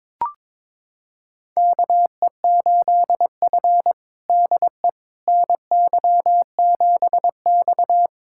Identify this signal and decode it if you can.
Morse code, heard as a keyed tone, about 22 wpm, 700 Hz. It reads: KE8F DE NY7X
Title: USA Callsign DE Callsign - 4 Characters - 22wpm